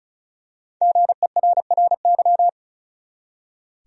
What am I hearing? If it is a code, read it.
Morse code, heard as a keyed tone, about 36 wpm, 700 Hz. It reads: GERRY
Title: USA Common Names - Max 5 Letters - 35wpm